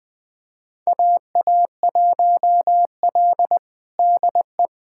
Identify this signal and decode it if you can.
Morse code, heard as a keyed tone, about 20 wpm, 700 Hz. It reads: AA1L DE
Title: USA Callsign DE Callsign - 4 Characters - 20wpm